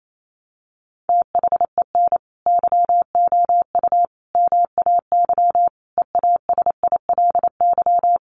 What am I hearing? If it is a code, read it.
Morse code, heard as a keyed tone, about 28 wpm, 700 Hz. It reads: THEN YOU MAY EASILY
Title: Sentences from Top 1200 Words 28wpm